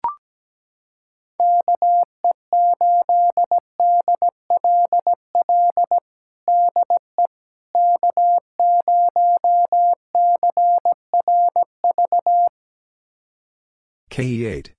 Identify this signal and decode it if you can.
Morse code, heard as a keyed tone, about 17 wpm, 700 Hz. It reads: KE8DLL DE K0CRV
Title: USA Callsign DE Callsign 17wpm